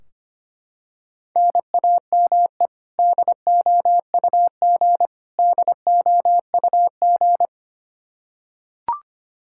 Morse code, heard as a keyed tone, about 25 wpm, 700 Hz. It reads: NAME DOUG DOUG
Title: Practice QSO - Exchange 1 - Beginning - 25wpm